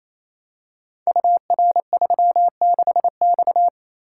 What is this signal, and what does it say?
Morse code, heard as a keyed tone, about 27 wpm, 700 Hz. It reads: UR36X
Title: Code Groups - 5 Letter:Numbers - 28wpm